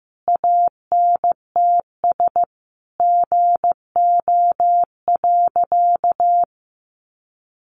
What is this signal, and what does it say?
Morse code, heard as a keyed tone, about 15 wpm, 700 Hz. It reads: ANTS GO.
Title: Additional Easy Phrases 15wpm